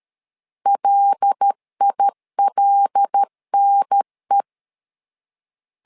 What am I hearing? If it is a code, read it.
Morse code, heard as a keyed tone, about 13 wpm, 800 Hz. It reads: LILNE
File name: LEZIONI / 5 LBVC / 5 L 12wpm 1913